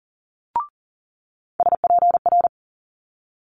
Morse code, heard as a keyed tone, about 38 wpm, 700 Hz. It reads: SPR
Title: Multi-letter Phonemes Quick Fire 40wpm